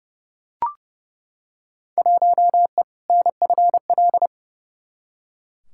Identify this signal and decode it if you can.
Morse code, heard as a keyed tone, about 30 wpm, 700 Hz. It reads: 1E NFL
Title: Field Day Exchanges - 30wpm